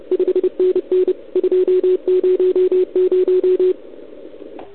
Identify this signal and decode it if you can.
Morse code, heard as a keyed tone, about 28 wpm, 400 Hz, with radio noise. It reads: 5NN 200
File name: ARCHIV / OKDX2019 / OKOM40 KX3